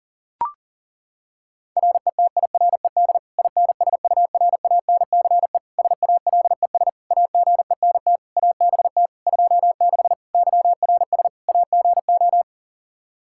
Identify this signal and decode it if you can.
Morse code, heard as a keyed tone, about 40 wpm, 700 Hz. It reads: RETIRED INSURANCE SALES AGENT ABT 26 YRS AGO
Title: Practice QSO - Exchange 3 - More Info - 40wpm